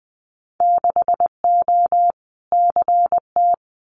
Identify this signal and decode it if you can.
Morse code, heard as a keyed tone, about 20 wpm, 700 Hz. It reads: 6O CT
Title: Winter Field Day Exchanges Encore 20wpm